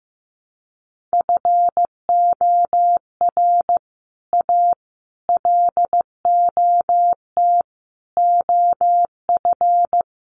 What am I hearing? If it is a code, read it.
Morse code, heard as a keyed tone, about 15 wpm, 700 Hz. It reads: FOR A LOT OF
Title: Sets of 5 Words Encore 15wpm